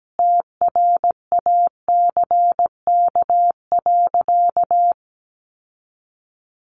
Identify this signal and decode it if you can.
Morse code, heard as a keyed tone, about 17 wpm, 700 Hz. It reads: TRACK.
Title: Sentences from Top 1100 Words 17wpm